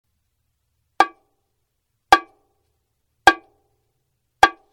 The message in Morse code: none